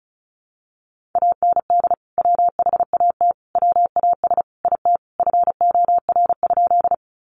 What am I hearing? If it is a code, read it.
Morse code, heard as a keyed tone, about 34 wpm, 700 Hz. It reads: AND WHAT WAS IT FOR?
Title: Sentences from Top 100 Words 35wpm